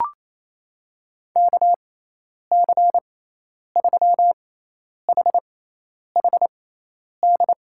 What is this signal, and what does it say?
Morse code, heard as a keyed tone, about 28 wpm, spaced as for 10 wpm, 700 Hz. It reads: KC3HHD
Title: USA Call Signs - 6 Characters - 6x Character Spacing - 28wpm